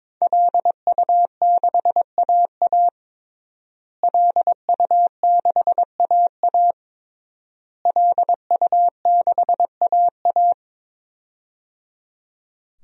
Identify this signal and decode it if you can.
Morse code, heard as a keyed tone, about 22 wpm, 700 Hz. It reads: LU6AA LU6AA LU6AA
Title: DXCC-100 Call Signs - 5 Characters - Repeated 3x - 22wpm